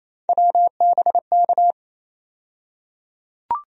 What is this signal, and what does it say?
Morse code, heard as a keyed tone, about 28 wpm, 700 Hz. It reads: WBK
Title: USA Call Sign Postfixes - 3 Letters 28wpm